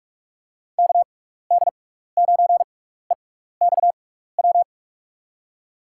Morse code, heard as a keyed tone, about 47 wpm, spaced as for 16 wpm, 700 Hz. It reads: KD9EXW
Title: USA Call Signs - 6 Characters - 6x Character Spacing - 45wpm